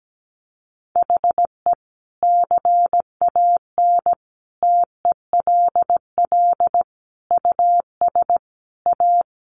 Morse code, heard as a keyed tone, about 17 wpm, 700 Hz. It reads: HE CAN TELL US A
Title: Sentences from Top 1300 Words 17wpm